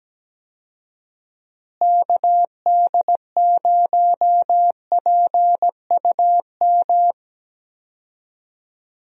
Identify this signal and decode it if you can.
Morse code, heard as a keyed tone, about 17 wpm, 700 Hz. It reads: KD0PUM
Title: USA Call Signs - 6 Characters - Repeated 3x - 17wpm